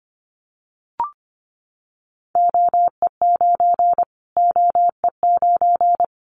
Morse code, heard as a keyed tone, about 25 wpm, 700 Hz. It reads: OE9 OE9
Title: Callsign Prefixes for Top 25 DXCC - Repeated 3x - 25wpm